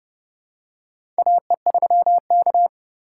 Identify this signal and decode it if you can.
Morse code, heard as a keyed tone, about 30 wpm, 700 Hz. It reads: AE3K
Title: USA Call Signs - 4x Character Spacing - 30wpm